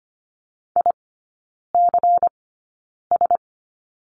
Morse code, heard as a keyed tone, about 25 wpm, spaced as for 9 wpm, 700 Hz. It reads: ICS
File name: course / 25:10wpm / 093 Lesson 16e - Letter c - Three Characters 25wpm